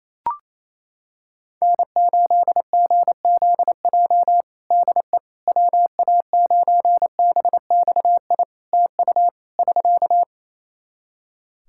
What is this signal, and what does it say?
Morse code, heard as a keyed tone, about 27 wpm, 700 Hz. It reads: N8GZJ DE WA9BXI TU <SK>
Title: K1USN SST QSO Practice 28wpm